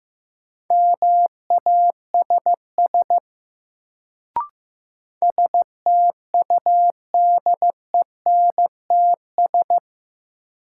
Morse code, heard as a keyed tone, about 15 wpm, 700 Hz. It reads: MASS STUDENTS
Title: Sets of 5 Words Encore - New Word Drill - 15wpm